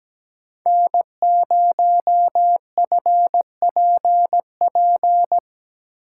Morse code, heard as a keyed tone, about 17 wpm, 700 Hz. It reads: N0FPP
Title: USA Call Signs - 5 Characters - 8x Character Spacing - 17wpm